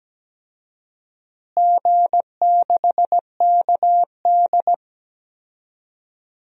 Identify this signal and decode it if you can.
Morse code, heard as a keyed tone, about 17 wpm, 700 Hz. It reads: G6KD
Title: DXCC-100 Call Signs - 4 Characters - Repeated 3x - 17wpm